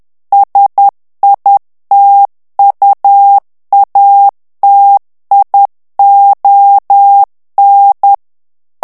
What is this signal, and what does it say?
Morse code, heard as a keyed tone, about 10 wpm, 800 Hz. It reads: SITUATION
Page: Français - texte en clair 1 - vitesse : 10 mots minute